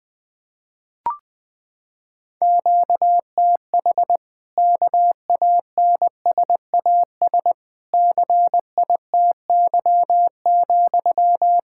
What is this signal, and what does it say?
Morse code, heard as a keyed tone, about 21 wpm, 700 Hz. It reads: QTH KANSAS CITY,
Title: Practice QSO - Exchange 1 - Beginning - 20wpm